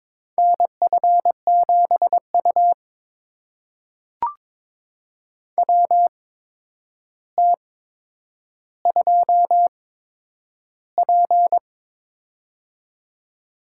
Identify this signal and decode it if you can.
Morse code, heard as a keyed tone, about 22 wpm, spaced as for 9 wpm, 700 Hz. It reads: NF7U WT2P
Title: USA Call Signs - 4 Characters - 8x Character Spacing - 22wpm